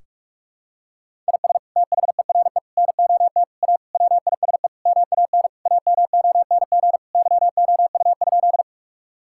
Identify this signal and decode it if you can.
Morse code, heard as a keyed tone, about 46 wpm, 700 Hz. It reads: IS THERE NOT A WISE MAN AMONG YOU?
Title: Sentences from Top 1600 Words 45wpm